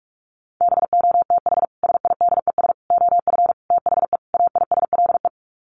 Morse code, heard as a keyed tone, about 44 wpm, 700 Hz. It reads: BOTH SIDES OF THE AISLE
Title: Sets of 5 Words Encore2 45wpm